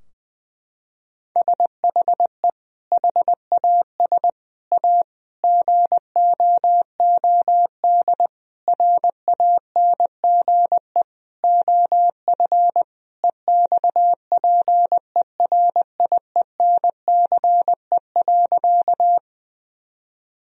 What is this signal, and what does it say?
Morse code, heard as a keyed tone, about 20 wpm, 700 Hz. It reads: SHE HAS A GOOD RANGE OF EXPERIENCE.
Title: Sentences from Top 1000 Words - Review Part 4 - 20wpm